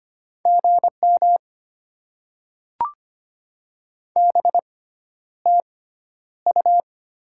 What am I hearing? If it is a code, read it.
Morse code, heard as a keyed tone, about 25 wpm, spaced as for 11 wpm, 700 Hz. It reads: GM BTU
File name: Top 20 QSO Elements - 6x Character Spacing - 25wpm